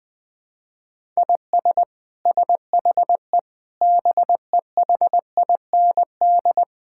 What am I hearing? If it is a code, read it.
Morse code, heard as a keyed tone, about 20 wpm, 700 Hz. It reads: IS SHE BEHIND
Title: Sentences from Top 1000 Words - Review Part 6 - 20wpm